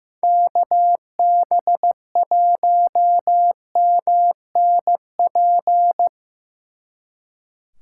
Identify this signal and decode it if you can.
Morse code, heard as a keyed tone, about 17 wpm, 700 Hz. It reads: KB1MNP
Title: Speed Racing - USA 2x3 Call Signs 15wpm